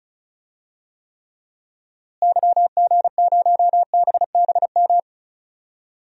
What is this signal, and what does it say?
Morse code, heard as a keyed tone, about 34 wpm, 700 Hz. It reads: YG0BBM
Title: DXCC-100 Call Signs - 6 Characters - Repeated 3x - 35wpm